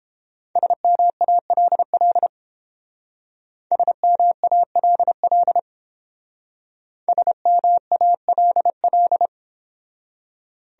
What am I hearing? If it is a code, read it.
Morse code, heard as a keyed tone, about 28 wpm, 700 Hz. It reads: SMALL SMALL SMALL